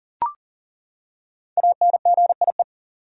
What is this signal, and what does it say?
Morse code, heard as a keyed tone, about 40 wpm, 700 Hz. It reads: ANGIE
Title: USA Common Names - 40wpm